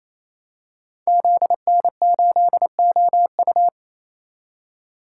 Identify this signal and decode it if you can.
Morse code, heard as a keyed tone, about 28 wpm, 700 Hz. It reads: ZN8OU
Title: Code Groups - 5 Letter:Numbers - 28wpm